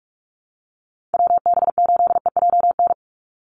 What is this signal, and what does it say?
Morse code, heard as a keyed tone, about 44 wpm, 700 Hz. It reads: WB8EJN
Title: Call Signs Repeated 3x - 45wpm